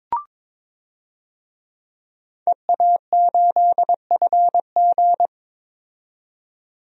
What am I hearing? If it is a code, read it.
Morse code, heard as a keyed tone, about 22 wpm, 700 Hz. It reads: EA8FG
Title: DXCC-100 Call Signs - 5 Characters - 22wpm